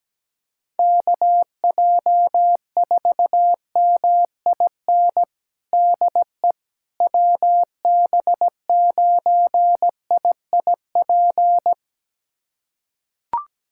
Morse code, heard as a keyed tone, about 17 wpm, 700 Hz. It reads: KJ4MIN DE WB9IIP
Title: USA Callsign DE Callsign - 6 Characters - 17wpm